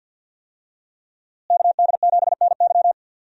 Morse code, heard as a keyed tone, about 52 wpm, 700 Hz. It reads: KD7NY
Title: USA Call Signs - 5 Characters - Repeated 3x - 50wpm